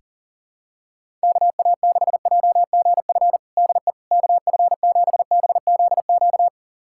Morse code, heard as a keyed tone, about 40 wpm, 700 Hz. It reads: KA6JGF DE KF7BZQ
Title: USA Callsign DE Callsign - 6 Characters - 40wpm